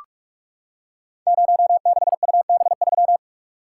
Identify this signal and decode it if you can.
Morse code, heard as a keyed tone, about 44 wpm, 700 Hz. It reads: 06UB3
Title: Code Groups - 5 Letter:Numbers - 45wpm